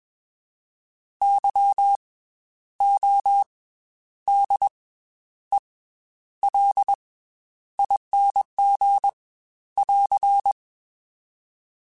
Morse code, heard as a keyed tone, about 21 wpm, 800 Hz. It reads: Y O D E L ING <AR>
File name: Special INGs / Special INGs-mp3s / Slow ing @20